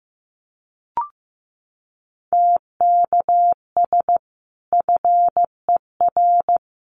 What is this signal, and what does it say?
Morse code, heard as a keyed tone, about 15 wpm, 700 Hz. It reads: TKS FER
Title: Common QSO Phrases 15wpm